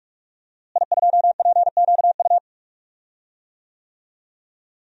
Morse code, heard as a keyed tone, about 42 wpm, 700 Hz. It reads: I2PQU
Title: DXCC-100 Call Signs - 5 Characters - Repeated 3x - 45wpm